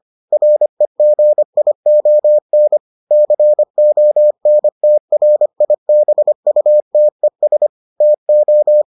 Morse code, heard as a keyed tone, about 25 wpm, 600 Hz. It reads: REGION CONTRIBUTES TO